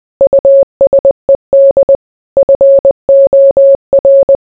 Morse code, heard as a keyed tone, about 20 wpm, 550 Hz. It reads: USED FOR